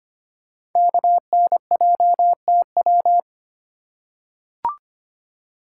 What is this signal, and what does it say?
Morse code, heard as a keyed tone, about 25 wpm, 700 Hz. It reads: KNJTW
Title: Code Groups - 5 Letter:Numbers - 25wpm